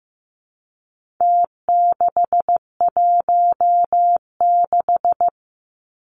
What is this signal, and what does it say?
Morse code, heard as a keyed tone, about 15 wpm, 700 Hz. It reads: T616
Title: Numbers - Contest Style - 4-Digit 15wpm